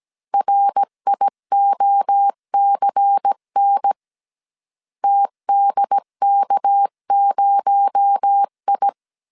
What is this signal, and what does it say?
Morse code, heard as a keyed tone, about 17 wpm, 800 Hz. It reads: RIOCN TDK0I
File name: CW / DET / 1 6 1500C S C 16wpm 2733